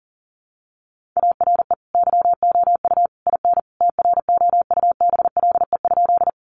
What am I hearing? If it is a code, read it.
Morse code, heard as a keyed tone, about 40 wpm, 700 Hz. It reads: ARE YOU IN TROUBLE?
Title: Sentences from Top 1200 Words 40wpm